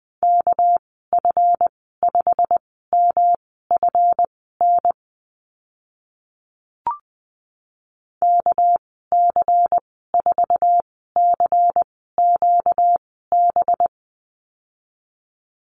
Morse code, heard as a keyed tone, about 20 wpm, spaced as for 14 wpm, 700 Hz. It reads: KF5MFN KC4CQB
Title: 245 Lesson 39g - Letter z - Call Signs 20wpm